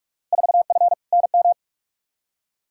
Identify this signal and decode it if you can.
Morse code, heard as a keyed tone, about 46 wpm, 700 Hz. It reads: 4F NM
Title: Field Day Exchanges Encore - 45wpm